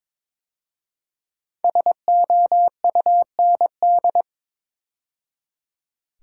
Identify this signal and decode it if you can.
Morse code, heard as a keyed tone, about 22 wpm, 700 Hz. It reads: SOUND
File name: 5 Letter Words - First 3 Letters Given - 22wpm